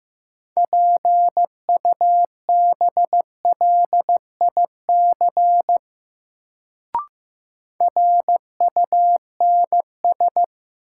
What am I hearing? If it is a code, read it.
Morse code, heard as a keyed tone, about 15 wpm, 700 Hz. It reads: PUBLIC RUNS
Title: Sentences from Top 600 Words - New Word Drill - 15wpm